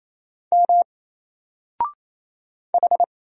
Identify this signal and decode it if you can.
Morse code, heard as a keyed tone, about 28 wpm, 700 Hz. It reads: M H
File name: Single-Letter-Number-28wpm